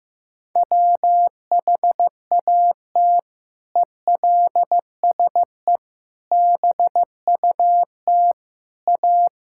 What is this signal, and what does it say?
Morse code, heard as a keyed tone, about 15 wpm, 700 Hz. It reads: WHAT ELSE BUT A
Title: Sentences from Top 1600 Words 15wpm